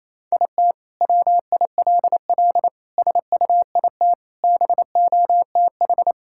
Morse code, heard as a keyed tone, about 28 wpm, 700 Hz. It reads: IT WILL SUIT BOTH
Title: Sentences from Top 1600 Words 28wpm